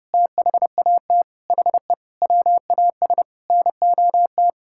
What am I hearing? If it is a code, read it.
Morse code, heard as a keyed tone, about 30 wpm, 700 Hz. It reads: THAT HE WAS NOT
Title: Sentences from Top 1600 Words 30wpm